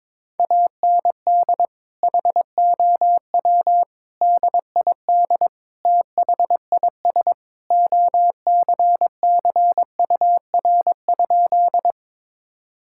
Morse code, heard as a keyed tone, about 22 wpm, 700 Hz. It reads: AND HOW DID THIS OCCUR?
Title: Sentences from Top 1000 Words - Review Part 6 - 22wpm